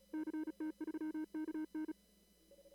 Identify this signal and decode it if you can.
Morse code, heard as a keyed tone, about 32 wpm, 310 Hz. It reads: CT3KN
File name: CW / kt5j wpx cw-op-k5pi